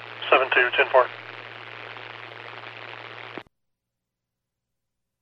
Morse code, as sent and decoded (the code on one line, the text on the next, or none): none